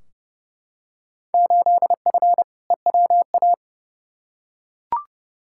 Morse code, heard as a keyed tone, about 30 wpm, 700 Hz. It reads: 8F EWA
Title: Field Day Exchanges Encore - 30wpm